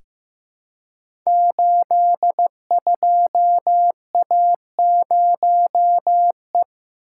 Morse code, heard as a keyed tone, about 15 wpm, 700 Hz. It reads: 82A0E
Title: Code Groups - 5 Letter:Numbers - 15wpm